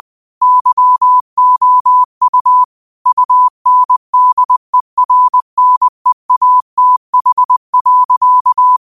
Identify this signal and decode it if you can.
Morse code, heard as a keyed tone, about 20 wpm, 1000 Hz. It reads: YOU UNDERNEATH.